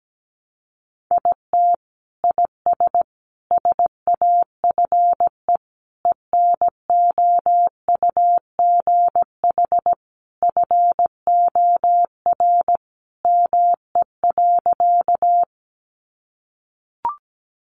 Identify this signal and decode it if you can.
Morse code, heard as a keyed tone, about 17 wpm, 700 Hz. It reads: IT IS SAFE ENOUGH FOR ME.
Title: Sentences from Top 900 Words 17wpm